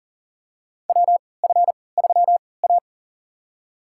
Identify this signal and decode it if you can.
Morse code, heard as a keyed tone, about 40 wpm, spaced as for 23 wpm, 700 Hz. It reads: WF3A
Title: USA Call Signs - 3x Character Spacing - 40wpm